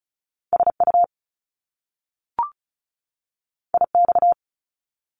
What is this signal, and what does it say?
Morse code, heard as a keyed tone, about 36 wpm, 700 Hz. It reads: SU IX
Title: Top Two Letter Combinations in Words 35wpm